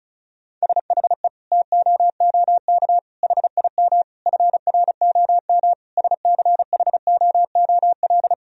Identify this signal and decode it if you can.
Morse code, heard as a keyed tone, about 36 wpm, 700 Hz. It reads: SHE TOOK HIM FROM SCHOOL
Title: Sentences from Top 1000 Words - Review Part 4 - 35wpm